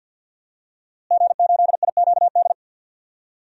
Morse code, heard as a keyed tone, about 52 wpm, 700 Hz. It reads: G8IQD
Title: DXCC-100 Call Signs - 5 Characters - Repeated 2x - 50wpm